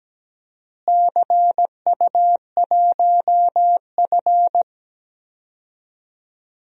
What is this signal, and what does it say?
Morse code, heard as a keyed tone, about 17 wpm, 700 Hz. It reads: CU1F
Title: DXCC-100 Call Signs - 4 Characters - Repeated 2x - 17wpm